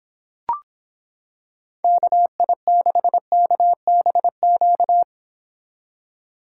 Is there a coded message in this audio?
Yes: Morse code from 1.8 to 5.0 s, heard as a keyed tone, about 26 wpm, 700 Hz. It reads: KI6KBQ